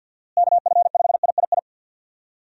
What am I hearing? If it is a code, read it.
Morse code, heard as a keyed tone, about 51 wpm, 700 Hz. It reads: KF5III
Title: USA Call Signs - 6 Characters - Repeated 3x - 50wpm